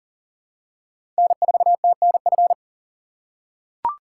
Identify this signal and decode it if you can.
Morse code, heard as a keyed tone, about 44 wpm, 690 Hz. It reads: N4TNF
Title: USA Call Signs - 5 Characters - 6x Character Spacing - 40wpm